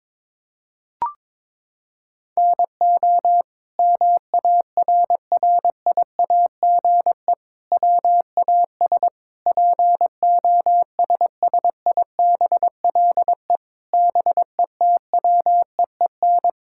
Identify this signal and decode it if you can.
Morse code, heard as a keyed tone, about 22 wpm, 700 Hz. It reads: NO MARRIAGE WAS POSSIBLE BETWEEN
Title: Sentences from Top 1000 Words - Review Part 5 - 22wpm